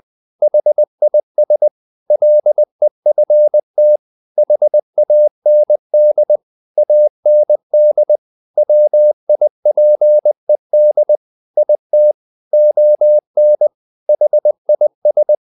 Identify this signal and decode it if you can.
Morse code, heard as a keyed tone, about 20 wpm, 600 Hz. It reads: HIS LEFT HAND AND WIPED IT ON HIS
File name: the old man and the sea part 2 20wpm